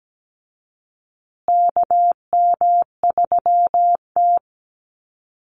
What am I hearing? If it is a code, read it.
Morse code, heard as a keyed tone, about 17 wpm, 700 Hz. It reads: KM3T
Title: Call Signs Repeated 2x - 17wpm